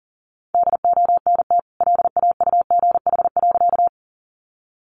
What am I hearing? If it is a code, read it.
Morse code, heard as a keyed tone, about 40 wpm, 700 Hz. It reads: DONT LAUGH.
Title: Sentences from Top 500 Words 40wpm